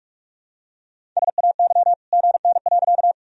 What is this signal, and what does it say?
Morse code, heard as a keyed tone, about 46 wpm, 700 Hz. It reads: SAY GN.
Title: Sentences from Top 300 Words and Abbreviations 45wpm